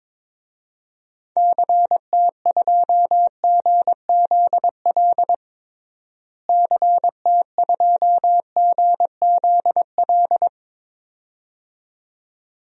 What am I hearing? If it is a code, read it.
Morse code, heard as a keyed tone, about 22 wpm, 700 Hz. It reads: CT2GZL CT2GZL
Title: DXCC-100 Call Signs - 6 Characters - Repeated 2x - 22wpm